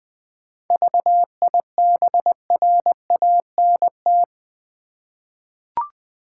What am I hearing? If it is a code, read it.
Morse code, heard as a keyed tone, about 20 wpm, 700 Hz. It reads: VIBRANT
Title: 7 Letter Words - First 3 Letters Given - 20wpm